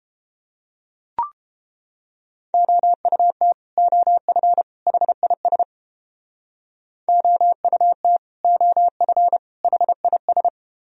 Morse code, heard as a keyed tone, about 29 wpm, 700 Hz. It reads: OUT OF HIS OUT OF HIS